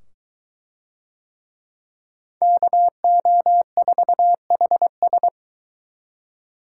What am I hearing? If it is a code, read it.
Morse code, heard as a keyed tone, about 23 wpm, 700 Hz. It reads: KO4HS